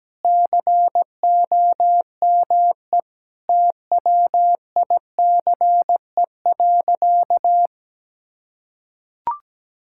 Morse code, heard as a keyed tone, about 17 wpm, 700 Hz. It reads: COME TWICE.
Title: Sentences from Top 1600 Words 17wpm